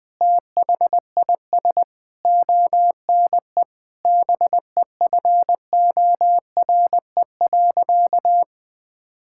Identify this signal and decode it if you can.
Morse code, heard as a keyed tone, about 20 wpm, 700 Hz. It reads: THIS ONE BEFORE.
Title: Sentences with 8-Word Phrases - 20wpm